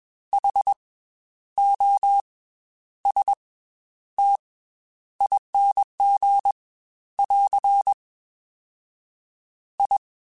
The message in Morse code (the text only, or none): H O S T ING <AR> I